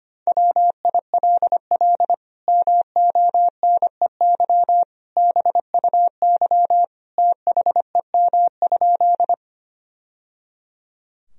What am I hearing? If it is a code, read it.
Morse code, heard as a keyed tone, about 25 wpm, 700 Hz. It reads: WILL MONEY BUY THEM?
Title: Sentences from Top 1000 Words - Review Part 4 - 25wpm